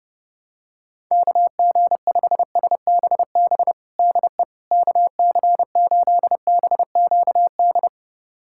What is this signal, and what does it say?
Morse code, heard as a keyed tone, about 32 wpm, 690 Hz. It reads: KG5SBB DE KC8BQD
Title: USA Callsign DE Callsign - 6 Characters - 30wpm